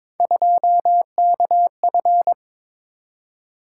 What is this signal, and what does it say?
Morse code, heard as a keyed tone, about 22 wpm, 700 Hz. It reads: 2KF
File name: practice / 22wpm / Three Letter-Number Combinations - 22wpm